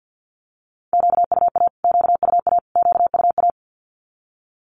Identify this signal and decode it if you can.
Morse code, heard as a keyed tone, about 50 wpm, 700 Hz. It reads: QUA QUA QUA
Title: Q-Codes Repeated 3x 50wpm